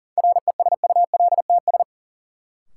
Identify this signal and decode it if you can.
Morse code, heard as a keyed tone, about 40 wpm, 700 Hz. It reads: RESULTS
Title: Sets of 4 Words - New Word Drill - 40wpm